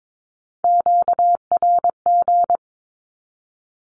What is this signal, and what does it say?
Morse code, heard as a keyed tone, about 22 wpm, 700 Hz. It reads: QRG
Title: Q-Codes with Spoken Meaning - 22wpm